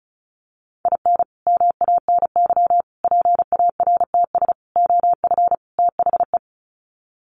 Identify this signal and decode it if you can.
Morse code, heard as a keyed tone, about 34 wpm, 700 Hz. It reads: IN MANY PARTS OF THE
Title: Sets of 5 Words Encore 35wpm